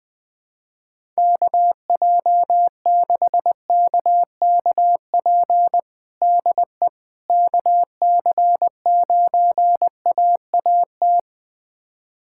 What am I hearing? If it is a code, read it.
Morse code, heard as a keyed tone, about 20 wpm, 700 Hz. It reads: KJ6KKP DE KC9AAT
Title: USA Callsign DE Callsign - 6 Characters - 20wpm